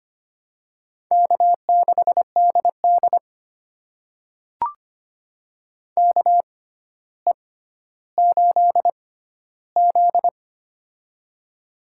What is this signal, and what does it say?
Morse code, heard as a keyed tone, about 25 wpm, spaced as for 13 wpm, 700 Hz. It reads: K6DD KE8Z
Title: USA Call Signs - 6x Character Spacing - 25wpm